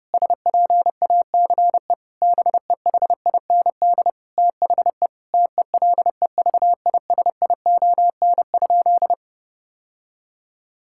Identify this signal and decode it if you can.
Morse code, heard as a keyed tone, about 30 wpm, 700 Hz. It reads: SPACE BEHIND THE TELEVISION?
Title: Sentences from Top 1000 Words - Review Part 5 - 30wpm